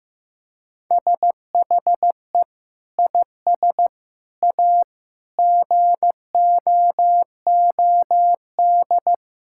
Morse code, heard as a keyed tone, about 15 wpm, 700 Hz. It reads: SHE IS A GOOD
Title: Sentences from Top 1600 Words 15wpm